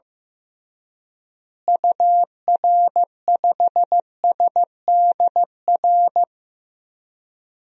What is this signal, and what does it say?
Morse code, heard as a keyed tone, about 15 wpm, 700 Hz. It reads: UR5SDR
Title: DXCC-100 Call Signs - 6 Characters - Repeated 3x - 15wpm